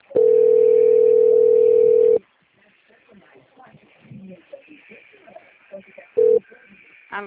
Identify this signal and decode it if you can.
Ringback tone (North America), 440 + 480 Hz, with speech.
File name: out- - - - - .20294